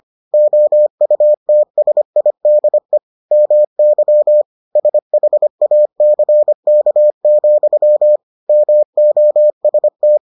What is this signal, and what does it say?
Morse code, heard as a keyed tone, about 25 wpm, 600 Hz. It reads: OUTSIDE MY SHACK, MOST